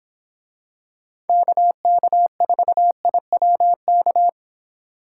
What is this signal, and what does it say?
Morse code, heard as a keyed tone, about 26 wpm, 700 Hz. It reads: KK4IWK